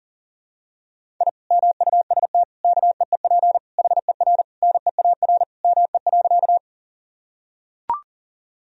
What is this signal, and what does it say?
Morse code, heard as a keyed tone, about 40 wpm, 700 Hz. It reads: I MUST KEEP HER NEAR ME.
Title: Sentences from Top 1000 Words - Review Part 1 - 40wpm